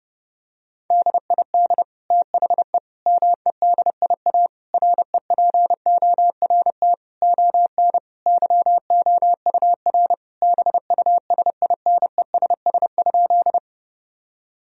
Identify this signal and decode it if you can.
Morse code, heard as a keyed tone, about 30 wpm, 700 Hz. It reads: DID THE MEDIA REPORT ON YOUR BUSINESS?
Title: Sentences from Top 1000 Words - Review Part 4 - 30wpm